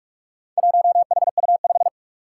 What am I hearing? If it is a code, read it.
Morse code, heard as a keyed tone, about 41 wpm, 700 Hz. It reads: 1HU5
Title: Four Challenging Letter-Number Combinations 45wpm